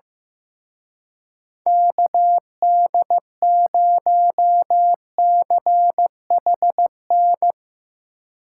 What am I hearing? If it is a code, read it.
Morse code, heard as a keyed tone, about 16 wpm, 700 Hz. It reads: KD0CHN
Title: Speed Racing - USA 2x3 Call Signs 15wpm